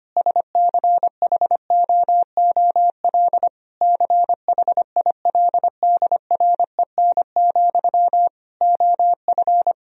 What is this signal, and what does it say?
Morse code, heard as a keyed tone, about 25 wpm, 700 Hz. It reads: SCHOOL CHILDREN, OF
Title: Sentences from Top 1000 Words - Review Part 1 - 25wpm